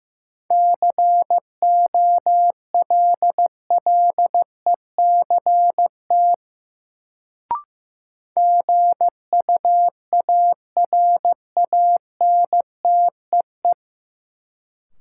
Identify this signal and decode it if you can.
Morse code, heard as a keyed tone, about 15 wpm, 700 Hz. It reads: COLLECT GUARANTEE
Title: Sentences from Top 1600 Words - New Word Drill Alt - 15wpm